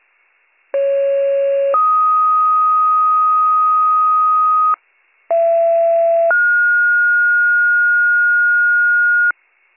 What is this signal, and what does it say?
Two-tone page, heard as a one-second tone, then a three-second tone, about 550 then 1250 Hz, about 670 then 1430 Hz, with noise. It is heard from 0.7 to 9.3 s.